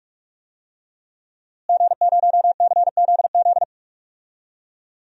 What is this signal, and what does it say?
Morse code, heard as a keyed tone, about 46 wpm, 700 Hz. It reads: G0CZZ
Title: DXCC-100 Call Signs - 5 Characters - Repeated 3x - 45wpm